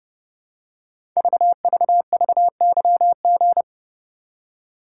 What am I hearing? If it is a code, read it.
Morse code, heard as a keyed tone, about 29 wpm, 700 Hz. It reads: VVVYG
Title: Code Groups - 5 Letters - 30wpm